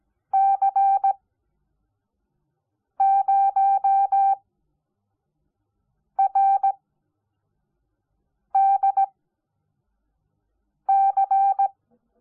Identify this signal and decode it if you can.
Morse code, heard as a keyed tone, about 17 wpm, spaced as for 4 wpm, 780 Hz. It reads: C0RDC
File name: Onlinelehrgang / morsen / DJ4UF Morsen05